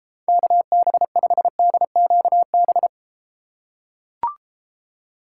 Morse code, heard as a keyed tone, about 29 wpm, 700 Hz. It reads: KB5DQB